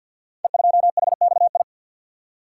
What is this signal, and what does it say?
Morse code, heard as a keyed tone, about 45 wpm, 700 Hz. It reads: E2HXI
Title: Code Groups - 5 Letter:Numbers - 50wpm